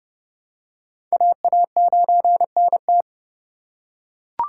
Morse code, heard as a keyed tone, about 29 wpm, 700 Hz. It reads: AA9NT